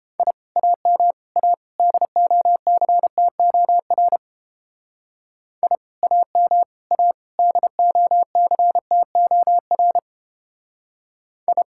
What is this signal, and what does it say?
Morse code, heard as a keyed tone, about 29 wpm, 700 Hz. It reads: I AM A DOCTOR I AM A DOCTOR I